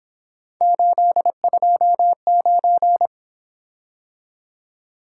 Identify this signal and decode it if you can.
Morse code, heard as a keyed tone, about 26 wpm, 700 Hz. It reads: 829
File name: Speed-Racing - Three Letter-Number Combinations - 22wpm